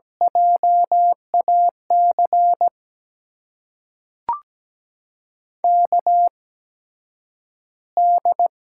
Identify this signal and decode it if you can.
Morse code, heard as a keyed tone, about 17 wpm, spaced as for 9 wpm, 700 Hz. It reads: JAC KD
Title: USA Call Signs - 6 Characters - 8x Character Spacing - 17wpm